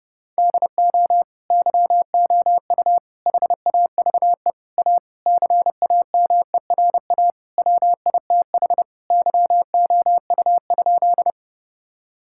Morse code, heard as a keyed tone, about 30 wpm, 700 Hz. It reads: DO YOU HAVE A CAMERA WITH YOU?
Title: Sentences from Top 1000 Words - Review Part 5 - 30wpm